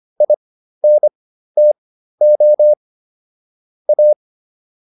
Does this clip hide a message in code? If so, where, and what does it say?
Yes: Morse code from 0.2 to 4.1 s, heard as a keyed tone, about 25 wpm, spaced as for 13 wpm, 600 Hz. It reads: INTO A